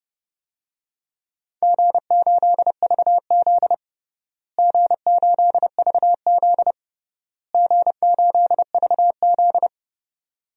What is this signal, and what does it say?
Morse code, heard as a keyed tone, about 29 wpm, 700 Hz. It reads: G8VZ G8VZ G8VZ
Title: DXCC-100 Call Signs - 4 Characters - Repeated 3x - 30wpm